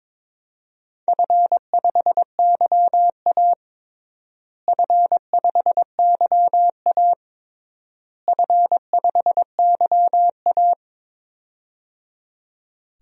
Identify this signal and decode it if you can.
Morse code, heard as a keyed tone, about 21 wpm, 700 Hz. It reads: F5YA F5YA F5YA
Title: DXCC-100 Call Signs - 4 Characters - Repeated 3x - 22wpm